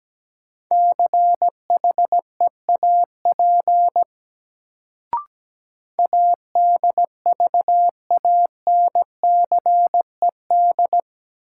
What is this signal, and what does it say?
Morse code, heard as a keyed tone, about 17 wpm, 700 Hz. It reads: CHEAP ADVANCED
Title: Sentences from Top 1200 Words - New Word Drill - 17wpm